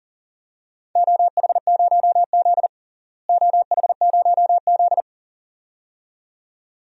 Morse code, heard as a keyed tone, about 37 wpm, 700 Hz. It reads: OH0Z OH0Z
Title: DXCC-100 Call Signs - 4 Characters - Repeated 2x - 40wpm